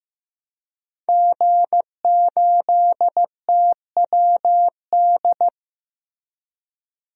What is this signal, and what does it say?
Morse code, heard as a keyed tone, about 16 wpm, 700 Hz. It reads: G8TWD
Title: DXCC-100 Call Signs - 5 Characters - Repeated 2x - 15wpm